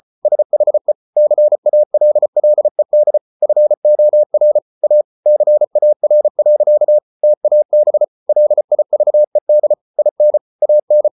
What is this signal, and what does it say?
Morse code, heard as a keyed tone, about 34 wpm, 600 Hz. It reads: SHE CALLED FOR A CAR. TAB LIVED IN AN